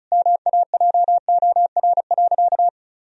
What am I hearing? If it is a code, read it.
Morse code, heard as a keyed tone, about 34 wpm, 700 Hz. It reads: MAJOR.
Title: Sentences from Top 1000 Words - Review Part 4 - 35wpm